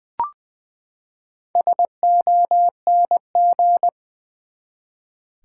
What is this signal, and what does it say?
Morse code, heard as a keyed tone, about 20 wpm, 700 Hz. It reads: SONG
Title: Four Letter Words - 20wpm